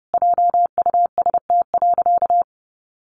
Morse code, heard as a keyed tone, about 30 wpm, 700 Hz. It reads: JUST.
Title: Sentences from Top 1400 Words 30wpm